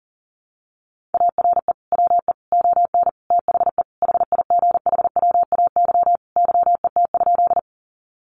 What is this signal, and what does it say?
Morse code, heard as a keyed tone, about 40 wpm, 700 Hz. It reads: ARE WE ON THE HIGHWAY YET?
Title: Sentences from Top 1400 Words 40wpm